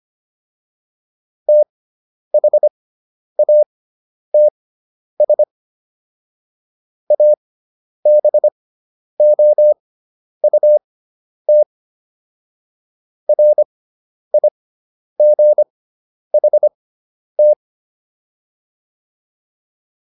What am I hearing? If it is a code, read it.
Morse code, heard as a keyed tone, about 25 wpm, spaced as for 10 wpm, 600 Hz. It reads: THATS ABOUT RIGHT